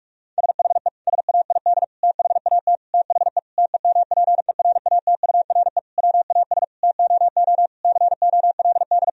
Morse code, heard as a keyed tone, about 46 wpm, 700 Hz. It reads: SHE SAID THAT THE TEMPERATURE WAS TOO COLD
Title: Sentences from Top 1000 Words - Review Part 4 - 45wpm